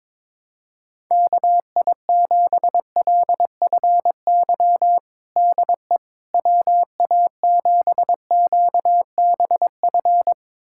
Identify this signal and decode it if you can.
Morse code, heard as a keyed tone, about 22 wpm, 700 Hz. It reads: KI7LFY DE WA7QBF
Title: USA Callsign DE Callsign 22wpm